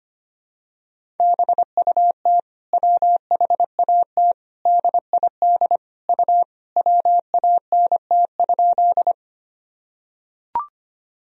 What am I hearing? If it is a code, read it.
Morse code, heard as a keyed tone, about 25 wpm, 700 Hz. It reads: BUT WHAT DID U WANT?
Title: Sentences from Top 200 Words and Abbreviations 25wpm